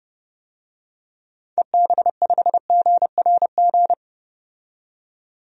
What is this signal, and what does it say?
Morse code, heard as a keyed tone, about 30 wpm, 700 Hz. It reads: EB5GRG
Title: DXCC-100 Call Signs - 6 Characters - 30wpm